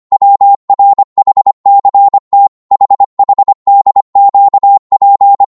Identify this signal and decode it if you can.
Morse code, heard as a keyed tone, about 25 wpm, 800 Hz. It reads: WRHCT HHDQP
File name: Morse / ltr-25wpm